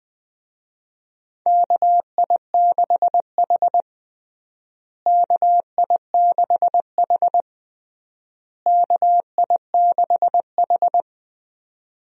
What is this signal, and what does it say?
Morse code, heard as a keyed tone, about 20 wpm, 700 Hz. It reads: KI6H KI6H KI6H
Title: USA Call Signs - 4 Characters - Repeated 3x - 20wpm